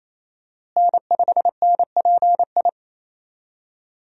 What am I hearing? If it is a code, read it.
Morse code, heard as a keyed tone, about 28 wpm, 700 Hz. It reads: N5NPI
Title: USA Call Signs - 5 Characters - Repeated 2x - 28wpm